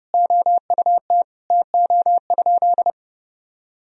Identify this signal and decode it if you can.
Morse code, heard as a keyed tone, about 30 wpm, 700 Hz. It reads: OUT TO?
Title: Sentences from Top 1000 Words - Review Part 3 - 30wpm